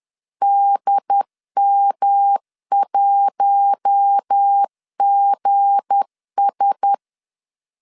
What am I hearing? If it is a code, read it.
Morse code, heard as a keyed tone, about 10 wpm, 800 Hz. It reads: DM1GS
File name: LEZIONI / 8 YJ45 / 8 YJ 10wpm 2651